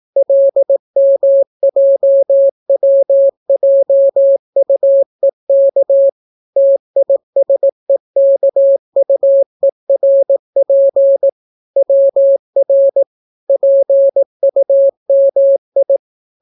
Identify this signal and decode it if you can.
Morse code, heard as a keyed tone, about 18 wpm, 550 Hz. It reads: LMJWJUEK TISEKUERP WR PUMI